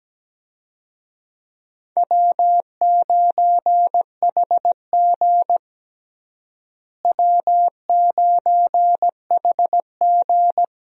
Morse code, heard as a keyed tone, about 17 wpm, 700 Hz. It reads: W9HG W9HG
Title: USA Call Signs - 4 Characters - Repeated 3x - 17wpm